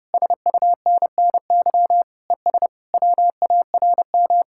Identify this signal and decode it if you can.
Morse code, heard as a keyed tone, about 36 wpm, 700 Hz. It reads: SUNNY ES WARM
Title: Speed-Racing Common QSO Phrases - 30wpm